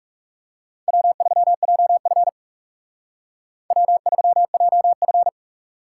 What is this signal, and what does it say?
Morse code, heard as a keyed tone, about 36 wpm, 700 Hz. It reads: W3JF W3JF
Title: Speed Racing - USA Call Signs 30wpm